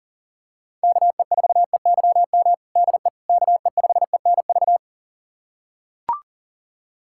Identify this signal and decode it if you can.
Morse code, heard as a keyed tone, about 40 wpm, 700 Hz. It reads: KE4EYM DE KE5ENV
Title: USA Callsign DE Callsign - 6 Characters - 40wpm